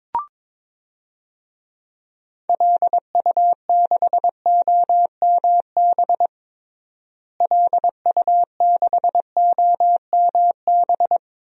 Morse code, heard as a keyed tone, about 22 wpm, 700 Hz. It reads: LU6OMB LU6OMB
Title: DXCC-100 Call Signs - 6 Characters - Repeated 3x - 22wpm